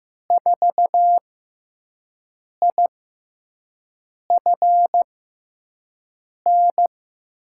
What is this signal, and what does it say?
Morse code, heard as a keyed tone, about 15 wpm, spaced as for 5 wpm, 700 Hz. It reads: 4IFN
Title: USA Call Signs - 6x Character Spacing - 15wpm